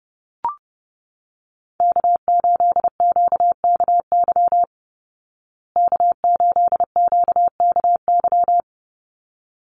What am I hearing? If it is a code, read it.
Morse code, heard as a keyed tone, about 30 wpm, 700 Hz. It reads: K8QKY K8QKY
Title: Call Signs Repeated 3x - 30wpm